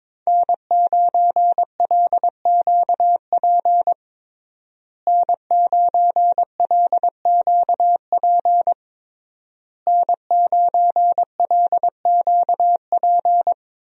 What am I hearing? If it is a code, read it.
Morse code, heard as a keyed tone, about 22 wpm, 700 Hz. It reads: N9LQP N9LQP N9LQP
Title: USA Call Signs - 5 Characters - Repeated 3x - 22wpm